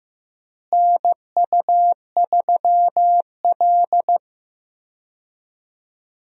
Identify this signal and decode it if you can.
Morse code, heard as a keyed tone, about 15 wpm, 700 Hz. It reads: NU3L
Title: USA Call Signs - 4 Characters - Repeated 3x - 15wpm